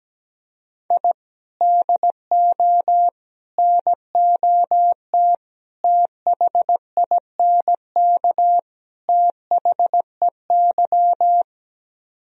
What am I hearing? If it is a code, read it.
Morse code, heard as a keyed tone, about 17 wpm, 700 Hz. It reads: I DO NOT THINK THEY
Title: Speed-Racing - Sets of 5 Words 17wpm